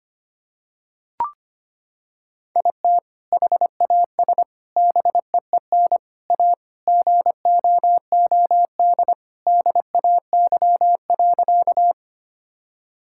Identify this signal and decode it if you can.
Morse code, heard as a keyed tone, about 25 wpm, 700 Hz. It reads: IT HAS BEEN A GOOD DAY.
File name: practice / 25wpm / Sentences with 2-Word Phrases - 25wpm